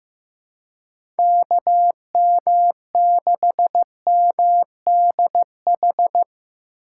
Morse code, heard as a keyed tone, about 15 wpm, 700 Hz. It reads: KM6MDH
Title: USA Call Signs - 6 Characters - 3x Character Spacing - 15wpm